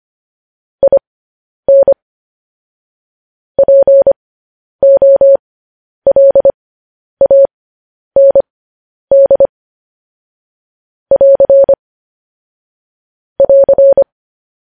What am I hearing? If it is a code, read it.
Morse code, heard as a keyed tone, about 25 wpm, spaced as for 10 wpm, 550 Hz. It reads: IN POLAND <AR> <AR>
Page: News Headlines in Morse Code at 10 WPM